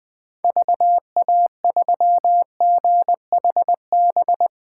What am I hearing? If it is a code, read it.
Morse code, heard as a keyed tone, about 21 wpm, 700 Hz. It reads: VA3GHB
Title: 2023 POTA Activator Call Signs - 20wpm